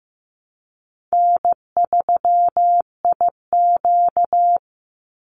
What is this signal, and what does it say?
Morse code, heard as a keyed tone, about 15 wpm, 700 Hz. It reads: N3IQ
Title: Call Signs Repeated 2x - 15wpm